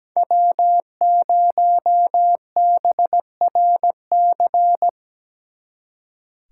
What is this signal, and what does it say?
Morse code, heard as a keyed tone, about 20 wpm, 700 Hz. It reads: W0BRC
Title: Speed Racing - USA 1x3 Call Signs 17wpm